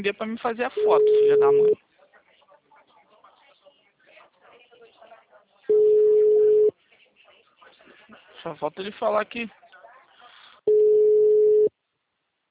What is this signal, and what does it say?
Ringback tone (Europe, 425 Hz), with speech.